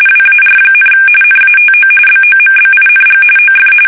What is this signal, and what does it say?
RTTY, about 45.45 baud. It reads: WAR, NATO BOMBING OF
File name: fsk850 / usnavy5